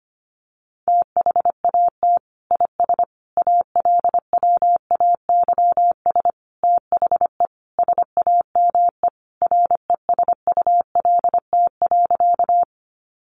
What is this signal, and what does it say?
Morse code, heard as a keyed tone, about 25 wpm, 700 Hz. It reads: THAT IS ALWAYS THE SAME RESULT.
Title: Sentences from Top 700 Words 25wpm